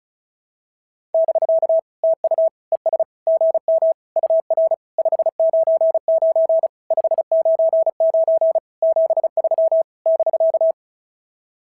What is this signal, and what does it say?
Morse code, heard as a keyed tone, about 36 wpm, 690 Hz. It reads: <BK> TU ES GM UR 599 599 73 <BK>
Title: SOTA QSO Practice 35wpm